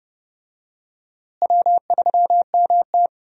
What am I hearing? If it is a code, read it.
Morse code, heard as a keyed tone, about 32 wpm, 700 Hz. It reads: W3MT
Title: USA Call Signs - 4 Characters - Repeated 3x - 30wpm